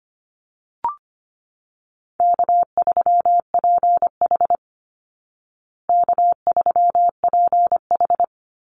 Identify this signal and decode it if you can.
Morse code, heard as a keyed tone, about 25 wpm, 700 Hz. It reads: K3PH K3PH
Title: Easy Call Signs Repeated 2x - 25wpm